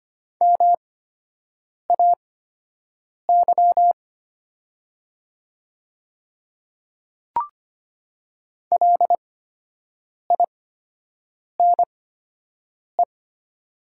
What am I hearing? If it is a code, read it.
Morse code, heard as a keyed tone, about 25 wpm, spaced as for 7 wpm, 700 Hz. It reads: MAY LINE
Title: 8x Character Spacing - Top 200 Words - 25wpm